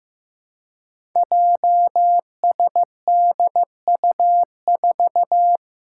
Morse code, heard as a keyed tone, about 15 wpm, 700 Hz. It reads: JSDU4
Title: Five Challenging Letter-Number Combinations 15wpm